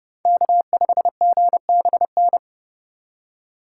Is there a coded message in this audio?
Yes: Morse code from 0.3 to 2.4 s, heard as a keyed tone, about 29 wpm, 700 Hz. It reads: K5GBN